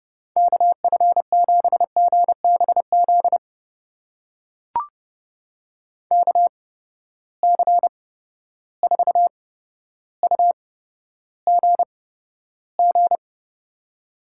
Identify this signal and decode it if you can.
Morse code, heard as a keyed tone, about 30 wpm, spaced as for 13 wpm, 700 Hz. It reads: KF7GBZ KC4UGG
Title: USA Call Signs - 6 Characters - 8x Character Spacing - 30wpm